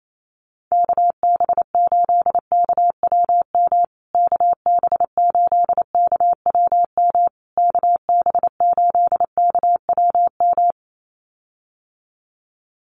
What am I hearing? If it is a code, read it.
Morse code, heard as a keyed tone, about 28 wpm, 700 Hz. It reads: KB8KWM KB8KWM KB8KWM
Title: USA Call Signs Repeated 3x 28wpm